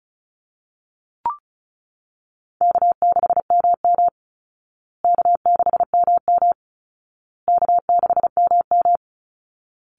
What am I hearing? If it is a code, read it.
Morse code, heard as a keyed tone, about 34 wpm, 700 Hz. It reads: K6MM K6MM K6MM
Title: Easy Call Signs Repeated 3x - 35wpm